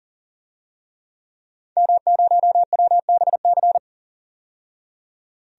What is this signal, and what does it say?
Morse code, heard as a keyed tone, about 40 wpm, 700 Hz. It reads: M0WBC
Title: DXCC-100 Call Signs - 5 Characters - 40wpm